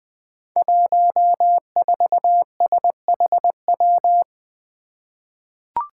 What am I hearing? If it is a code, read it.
Morse code, heard as a keyed tone, about 21 wpm, 700 Hz. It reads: 14SHW
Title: Five Challenging Letter-Number Combinations 20wpm